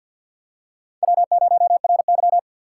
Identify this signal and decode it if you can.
Morse code, heard as a keyed tone, about 50 wpm, 700 Hz. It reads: W0RY
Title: USA Call Signs - 4 Characters - 6x Character Spacing - 50wpm